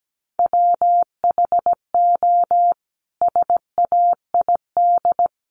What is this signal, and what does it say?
Morse code, heard as a keyed tone, about 17 wpm, 700 Hz. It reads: WHO SAID
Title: Sentences from Top 700 Words 17wpm